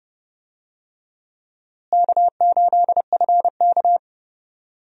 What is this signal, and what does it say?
Morse code, heard as a keyed tone, about 30 wpm, 700 Hz. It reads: K8FK
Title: USA Call Signs - 4 Characters - Repeated 2x - 30wpm